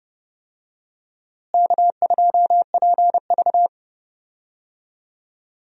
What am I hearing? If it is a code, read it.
Morse code, heard as a keyed tone, about 32 wpm, 700 Hz. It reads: K2PV
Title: USA Call Signs - 4 Characters - Repeated 3x - 30wpm